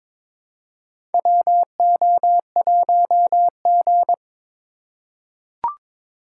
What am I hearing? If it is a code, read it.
Morse code, heard as a keyed tone, about 22 wpm, 700 Hz. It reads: WO1G
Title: USA Call Signs - 2x Character Spacing - 22wpm